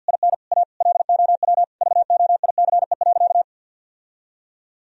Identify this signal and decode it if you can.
Morse code, heard as a keyed tone, about 49 wpm, 700 Hz. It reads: IN A LOW VOICE.
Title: Sentences from Top 1000 Words - Review Part 1 - 50wpm